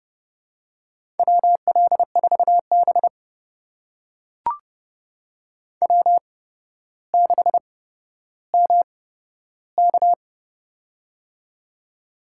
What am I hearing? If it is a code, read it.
Morse code, heard as a keyed tone, about 30 wpm, spaced as for 13 wpm, 700 Hz. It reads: WL4B W6MK
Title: USA Call Signs - 4 Characters - 8x Character Spacing - 30wpm